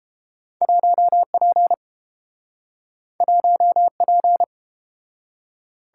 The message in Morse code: 1P 1P

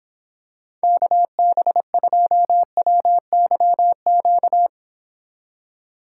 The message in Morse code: KB2WYQ

